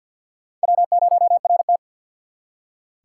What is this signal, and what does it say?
Morse code, heard as a keyed tone, about 66 wpm, spaced as for 57 wpm, 700 Hz. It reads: W0RT
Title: Speed Racing - USA 1x2 Call Signs 50wpm